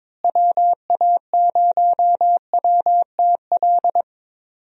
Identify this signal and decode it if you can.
Morse code, heard as a keyed tone, about 22 wpm, 700 Hz. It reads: WA0WTL
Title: USA Call Signs - 6 Characters - Repeated 2x - 22wpm